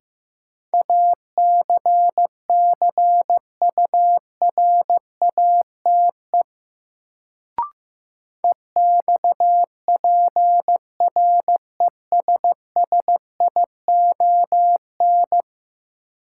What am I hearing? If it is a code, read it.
Morse code, heard as a keyed tone, about 15 wpm, 700 Hz. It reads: ACCURATE EXPRESSION
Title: Sentences from Top 1200 Words - New Word Drill Alt - 15wpm